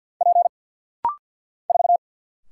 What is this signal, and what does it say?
Morse code, heard as a keyed tone, about 50 wpm, 700 Hz. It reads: P 4
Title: Single Letter-Number Alt - Rapid-Fire 50wpm